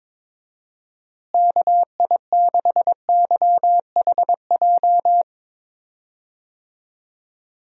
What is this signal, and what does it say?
Morse code, heard as a keyed tone, about 22 wpm, 700 Hz. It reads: KI6YHJ
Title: USA Call Signs - 6 Characters - Repeated 2x - 22wpm